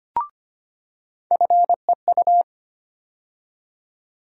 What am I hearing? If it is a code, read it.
Morse code, heard as a keyed tone, about 25 wpm, 700 Hz. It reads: FEU
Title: USA Call Sign Postfixes - 3 Letters 25wpm